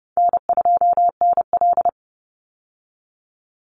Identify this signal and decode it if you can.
Morse code, heard as a keyed tone, about 30 wpm, 700 Hz. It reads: N2NL
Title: Call Signs Repeated 3x - 30wpm